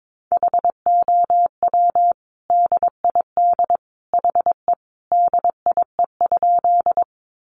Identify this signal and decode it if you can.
Morse code, heard as a keyed tone, about 22 wpm, 700 Hz. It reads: HOW DID HE DIE?
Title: Sentences from Top 1300 Words 22wpm